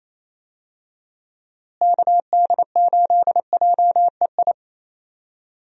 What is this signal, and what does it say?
Morse code, heard as a keyed tone, about 27 wpm, 700 Hz. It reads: KD8JEI
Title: USA Call Signs - 6 Characters - Repeated 3x - 28wpm